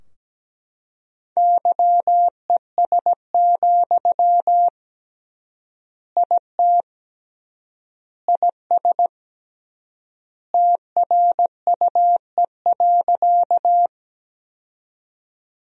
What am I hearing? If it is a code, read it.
Morse code, heard as a keyed tone, about 17 wpm, 700 Hz. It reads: YES, IT IS TRUE.
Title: 3x Word-Spacing - Sentences from Top 500 Words - 17wpm